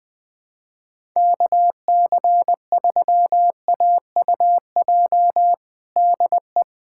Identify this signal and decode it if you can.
Morse code, heard as a keyed tone, about 21 wpm, 700 Hz. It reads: KC3AUJ DE
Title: USA Callsign DE Callsign - 6 Characters - 20wpm